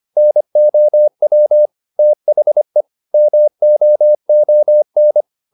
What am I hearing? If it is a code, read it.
Morse code, heard as a keyed tone, about 25 wpm, 600 Hz. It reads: NOW THE MOON
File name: the old man and the sea part 1 25wpm